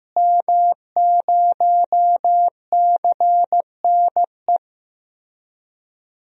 Morse code, heard as a keyed tone, about 15 wpm, 700 Hz. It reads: M0CNE
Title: DXCC-100 Call Signs - 5 Characters - Repeated 3x - 15wpm